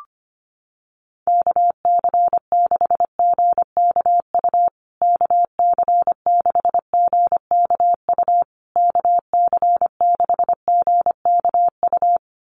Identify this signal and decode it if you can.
Morse code, heard as a keyed tone, about 25 wpm, 700 Hz. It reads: KC6GKU KC6GKU KC6GKU
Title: USA Call Signs Repeated 3x 25wpm